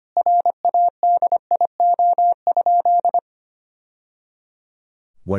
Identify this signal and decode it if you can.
Morse code, heard as a keyed tone, about 25 wpm, 700 Hz. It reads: RADIO?
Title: Sentences from Top 1000 Words - Review Part 3 - 25wpm